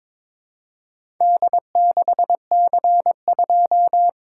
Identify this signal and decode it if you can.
Morse code, heard as a keyed tone, about 22 wpm, 700 Hz. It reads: D6C2
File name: practice / 22wpm / Four Letter-Number Combinations - 22wpm